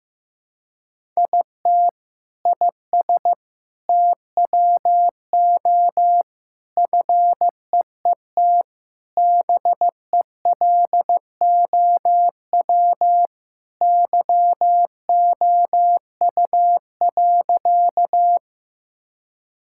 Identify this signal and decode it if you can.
Morse code, heard as a keyed tone, about 15 wpm, 700 Hz. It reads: IT IS TWO FEET BELOW YOU.
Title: Sentences from Top 1000 Words - Review Part 5 - 15wpm